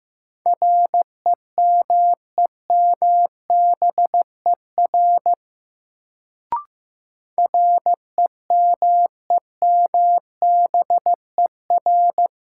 Morse code, heard as a keyed tone, about 15 wpm, 700 Hz. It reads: REMEMBER REMEMBER
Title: Sentences from Top 400 Words - New Word Drill - 15wpm